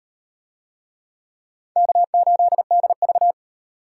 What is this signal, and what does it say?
Morse code, heard as a keyed tone, about 34 wpm, 700 Hz. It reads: K8DV